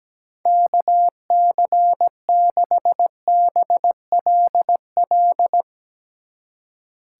Morse code, heard as a keyed tone, about 19 wpm, 700 Hz. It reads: KC6BLL
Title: Speed Racing - USA Call Signs 17wpm